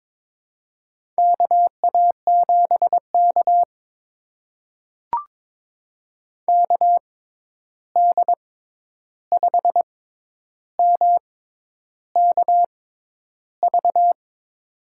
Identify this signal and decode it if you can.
Morse code, heard as a keyed tone, about 22 wpm, spaced as for 10 wpm, 700 Hz. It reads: KA7K KD5MKV
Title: USA Call Signs - 6x Character Spacing - 22wpm